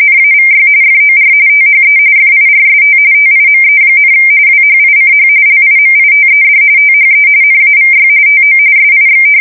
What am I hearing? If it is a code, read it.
RTTY, about 45.45 baud. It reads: NAVY FOR THE FIRST TIME, WHICH LIMITED THE NAVY TO SIX
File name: fsk170 / usnavy1